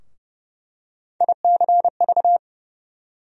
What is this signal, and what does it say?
Morse code, heard as a keyed tone, about 30 wpm, 700 Hz. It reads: ICV
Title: USA Call Sign Postfixes - 3 Letters 30wpm